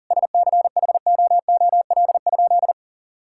Morse code, heard as a keyed tone, about 40 wpm, 700 Hz. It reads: SCHOOL?
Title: Sentences from Top 1000 Words - Review Part 5 - 40wpm